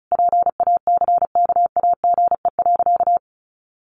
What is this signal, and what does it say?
Morse code, heard as a keyed tone, about 34 wpm, 700 Hz. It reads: PACKAGE.
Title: Easy Phrases 35wpm